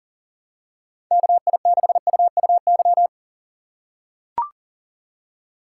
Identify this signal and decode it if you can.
Morse code, heard as a keyed tone, about 40 wpm, 700 Hz. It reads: KI6UUY
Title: USA Call Signs - 6 Characters - 3x Character Spacing - 40wpm